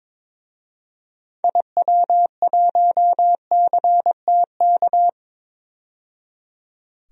Morse code, heard as a keyed tone, about 22 wpm, 700 Hz. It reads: IW1CTK
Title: DXCC-100 Call Signs - 6 Characters - 22wpm